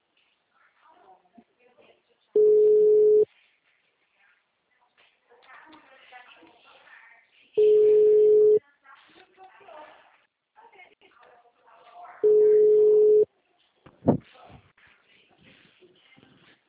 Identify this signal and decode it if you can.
Ringback tone (Europe, 425 Hz), with speech.